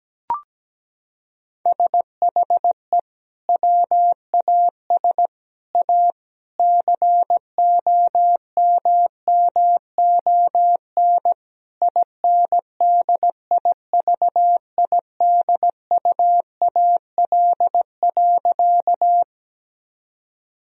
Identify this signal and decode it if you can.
Morse code, heard as a keyed tone, about 17 wpm, 700 Hz. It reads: SHE WAS A COMMON INDIVIDUAL.
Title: Sentences from Top 1000 Words - Review Part 4 - 17wpm